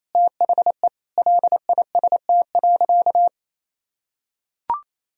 Morse code, heard as a keyed tone, about 27 wpm, 700 Hz. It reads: THE LIST.
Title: Sentences from Top 1000 Words - Review Part 2 - 28wpm